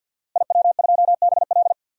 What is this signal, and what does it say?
Morse code, heard as a keyed tone, about 51 wpm, 700 Hz. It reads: IW2BL
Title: DXCC-100 Call Signs - 5 Characters - Repeated 3x - 50wpm